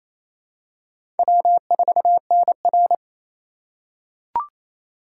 Morse code, heard as a keyed tone, about 27 wpm, 700 Hz. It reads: W4NR
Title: USA Call Signs - 4 Characters - 3x Character Spacing - 28wpm